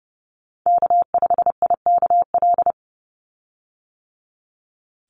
Morse code, heard as a keyed tone, about 28 wpm, 700 Hz. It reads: K5IKL
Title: USA Call Signs 30wpm